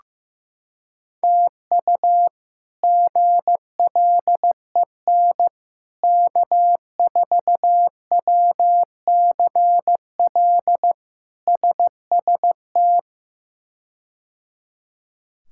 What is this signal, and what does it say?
Morse code, heard as a keyed tone, about 15 wpm, 700 Hz. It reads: TU GLEN K4WCL SST
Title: K1USN SST QSO Practice - 15wpm